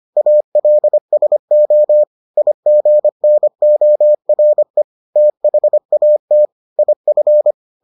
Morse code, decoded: ALSO IGNORE THAT IF

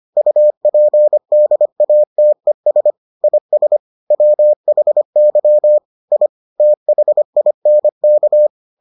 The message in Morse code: UPDATES IS WHY I THINK

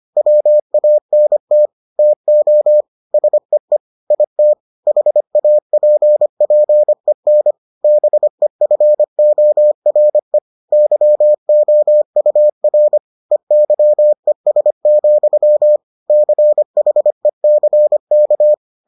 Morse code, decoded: WANT TO SEE IT HAPPEN BEFORE YOUR EYES, CHECK

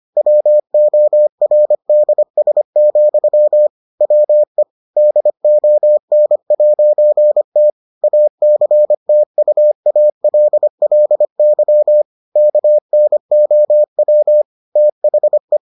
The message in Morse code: WORDS, WE DON'T ACTUALLY KNOW THE

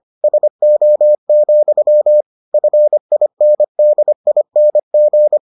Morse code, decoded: SO, FINDING

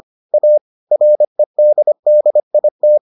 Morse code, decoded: A REDDIT

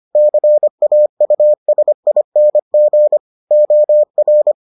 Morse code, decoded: CAUSING OR

